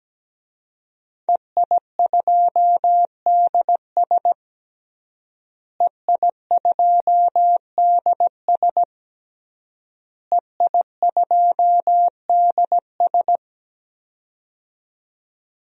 Morse code, heard as a keyed tone, about 17 wpm, 700 Hz. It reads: EI2DS EI2DS EI2DS
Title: DXCC-100 Call Signs - 5 Characters - Repeated 3x - 17wpm